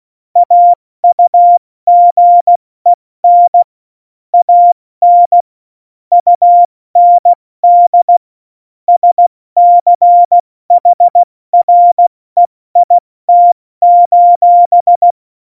Morse code, heard as a keyed tone, about 16 wpm, spaced as for 14 wpm, 700 Hz. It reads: AUGEN AN UND SCHREIT:
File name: Morsekurs / Witze / Witz 08. Dezember 2020, 70 F80